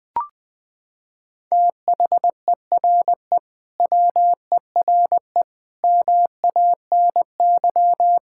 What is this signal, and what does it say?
Morse code, heard as a keyed tone, about 21 wpm, 700 Hz. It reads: THERE WERE MANY
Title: Sentences with 3-Word Phrases - 20wpm